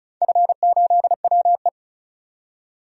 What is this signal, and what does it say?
Morse code, heard as a keyed tone, about 36 wpm, 700 Hz. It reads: F8WE
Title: DXCC-100 Call Signs - 4 Characters - 35wpm